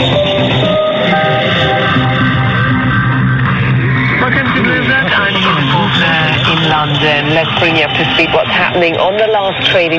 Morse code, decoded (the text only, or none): none